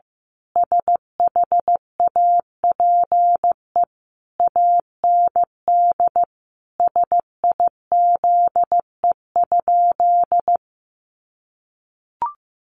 Morse code, heard as a keyed tone, about 15 wpm, 700 Hz. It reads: SHAPE AND SIZE?
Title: Sentences from Top 500 Words 15wpm